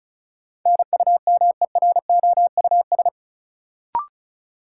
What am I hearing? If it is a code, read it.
Morse code, heard as a keyed tone, about 36 wpm, 700 Hz. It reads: NUMEROUS
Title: Sentences from Top 1400 Words - New Word Drill - 35wpm